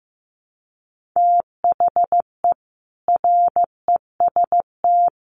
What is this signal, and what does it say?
Morse code, heard as a keyed tone, about 15 wpm, 700 Hz. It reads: THE REST
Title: Sets of 5 Words Encore 15wpm